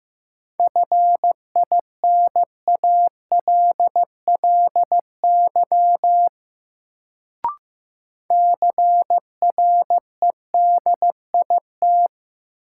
Morse code, heard as a keyed tone, about 15 wpm, 700 Hz. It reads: FINALLY CREDIT
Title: Sentences from Top 1000 Words - New Word Drill - 15wpm